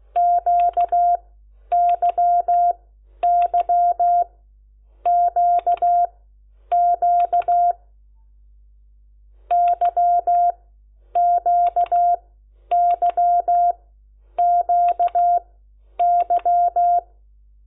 Morse code, heard as a keyed tone, about 16 wpm, spaced as for 9 wpm, 700 Hz. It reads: QYYQQ YQYQY
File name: Lecciones / CLASE Nº10